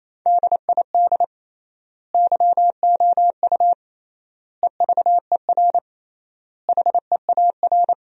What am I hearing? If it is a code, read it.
Morse code, heard as a keyed tone, about 28 wpm, 700 Hz. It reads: DID YOU EVER HEAR
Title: 3x Word-Spacing - Sentences from Top 400 Words - 28wpm